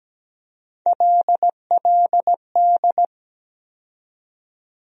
Morse code, heard as a keyed tone, about 17 wpm, 700 Hz. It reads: LLD
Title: USA Call Sign Postfixes - 3 Letters 17wpm